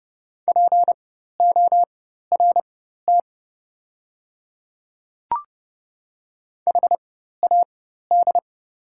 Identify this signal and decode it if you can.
Morse code, heard as a keyed tone, about 30 wpm, spaced as for 14 wpm, 690 Hz. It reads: PORT HAD
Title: 4x Character Spacing - Top 300 Words - 30wpm